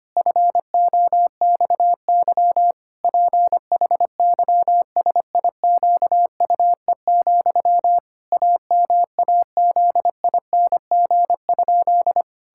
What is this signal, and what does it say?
Morse code, heard as a keyed tone, about 25 wpm, 700 Hz. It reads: FOXY PHYSIQUE, AMAZING?
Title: Pangram Challenge v2 - 25wpm